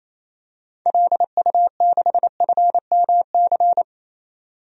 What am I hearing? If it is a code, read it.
Morse code, heard as a keyed tone, about 28 wpm, 700 Hz. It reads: LU6FMC
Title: DXCC-100 Call Signs - 6 Characters - 28wpm